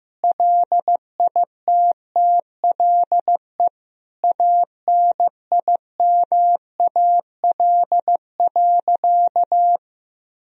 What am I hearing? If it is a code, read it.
Morse code, heard as a keyed tone, about 15 wpm, 700 Hz. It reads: LITTLE ANIMAL.
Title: Sentences from Top 1000 Words - Review Part 1 - 15wpm